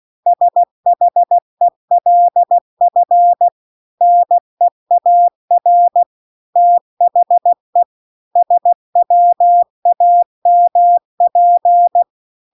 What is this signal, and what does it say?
Morse code, heard as a keyed tone, about 16 wpm, 700 Hz. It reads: SHELF NEAR THE SWAMP
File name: CW-The-Big-Twoheaded-River-2-16WPM